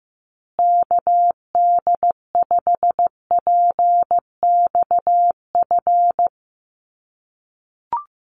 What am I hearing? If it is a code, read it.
Morse code, heard as a keyed tone, about 15 wpm, 700 Hz. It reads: KD5PXF
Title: USA Call Signs Repeated 3x 15wpm